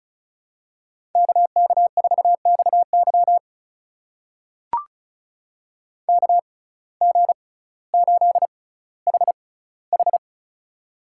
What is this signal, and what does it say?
Morse code, heard as a keyed tone, about 36 wpm, spaced as for 18 wpm, 700 Hz. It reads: KK4XY KG8HH
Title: USA Call Signs - 5 Characters - 6x Character Spacing - 35wpm